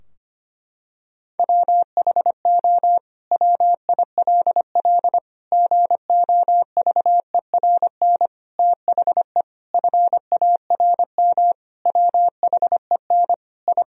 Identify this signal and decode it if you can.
Morse code, heard as a keyed tone, about 25 wpm, 700 Hz. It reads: WHO WILL GOVERN THE FARM WHEN I
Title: Sentences from Top 1000 Words - Review Part 2 - 25wpm